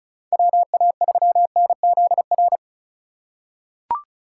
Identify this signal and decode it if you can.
Morse code, heard as a keyed tone, about 36 wpm, 700 Hz. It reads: WA3NZR
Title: K1USN SST QSO Practice - 35wpm